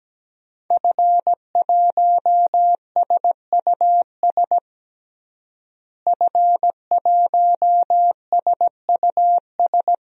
Morse code, heard as a keyed tone, about 17 wpm, 700 Hz. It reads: F1SUS F1SUS
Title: DXCC-100 Call Signs - 5 Characters - Repeated 3x - 17wpm